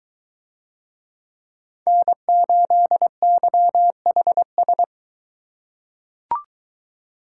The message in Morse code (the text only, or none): N8YHS